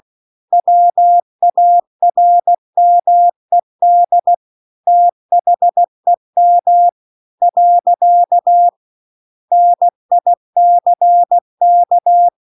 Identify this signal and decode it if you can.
Morse code, heard as a keyed tone, about 16 wpm, 700 Hz. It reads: WARMED THEM . NICK
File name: CW-The-Big-Twoheaded-River-2-16WPM